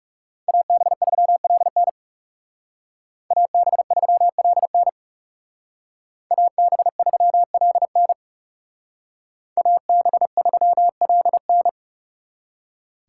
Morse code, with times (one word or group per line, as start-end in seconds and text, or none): 0.5-1.9 AB3LN
3.3-4.9 AB3LN
6.3-8.1 AB3LN
9.6-11.7 AB3LN